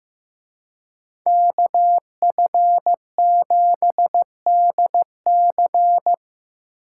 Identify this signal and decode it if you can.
Morse code, heard as a keyed tone, about 18 wpm, 700 Hz. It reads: KF7DC
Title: Speed Racing - USA 2x2 Call Signs 15wpm